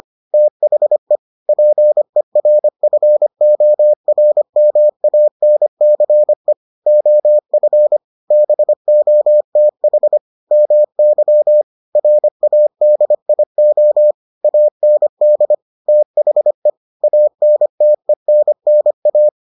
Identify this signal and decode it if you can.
Morse code, heard as a keyed tone, about 25 wpm, 600 Hz. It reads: THE PERFORMANCE OF BOTH MY RADIO AND THE ANTENNA